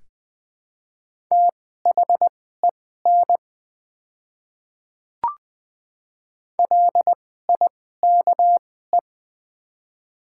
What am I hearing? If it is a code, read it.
Morse code, heard as a keyed tone, about 20 wpm, spaced as for 14 wpm, 700 Hz. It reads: THEN LIKE
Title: 2x Character Spacing - Top 100 Words - 20wpm